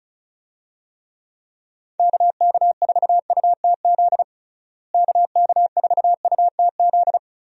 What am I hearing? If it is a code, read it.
Morse code, heard as a keyed tone, about 36 wpm, 700 Hz. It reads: KK4UTZ KK4UTZ
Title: USA Call Signs - 6 Characters - Repeated 2x - 35wpm